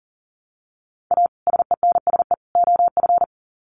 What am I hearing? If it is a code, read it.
Morse code, heard as a keyed tone, about 40 wpm, 700 Hz. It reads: A SENSE OF
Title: Sets of 3 Words 40wpm